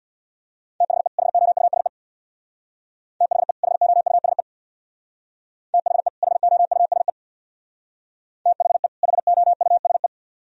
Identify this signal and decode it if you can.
Morse code, heard as a keyed tone, about 62 wpm, 700 Hz. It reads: THE HOUSE THE HOUSE THE HOUSE THE HOUSE
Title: Speed-Racing - Sets of 2 Words 50wpm